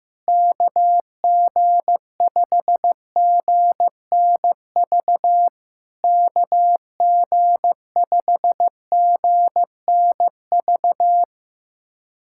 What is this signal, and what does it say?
Morse code, heard as a keyed tone, about 15 wpm, 700 Hz. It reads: KG5GNV KG5GNV
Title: SOTA QSO Practice 15wpm